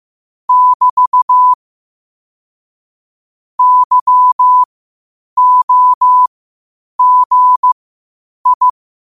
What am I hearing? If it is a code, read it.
Morse code, heard as a keyed tone, about 15 wpm, spaced as for 8 wpm, 1000 Hz. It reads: <BT> YOGI
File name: qotdfiles / 8wpm